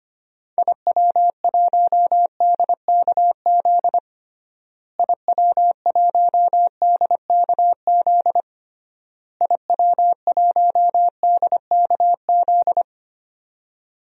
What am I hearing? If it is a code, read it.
Morse code, heard as a keyed tone, about 25 wpm, 700 Hz. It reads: IW1DKZ IW1DKZ IW1DKZ
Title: DXCC-100 Call Signs - 6 Characters - Repeated 3x - 25wpm